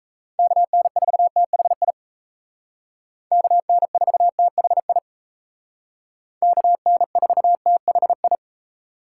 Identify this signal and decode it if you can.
Morse code, heard as a keyed tone, about 36 wpm, 700 Hz. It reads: KN4THI KN4THI KN4THI